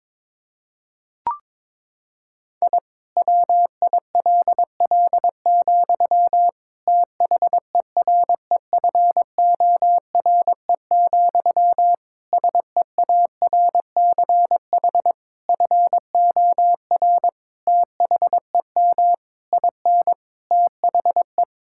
Morse code, heard as a keyed tone, about 22 wpm, 700 Hz. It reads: I WILL, THEREFORE, SEARCH FOR THEM IN THE
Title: Sentences from Top 1000 Words - Review Part 5 - 22wpm